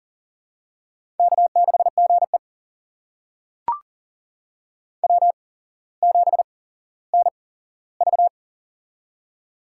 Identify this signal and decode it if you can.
Morse code, heard as a keyed tone, about 40 wpm, spaced as for 17 wpm, 700 Hz. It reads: K6GE W7NV
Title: USA Call Signs - 4 Characters - 8x Character Spacing - 40wpm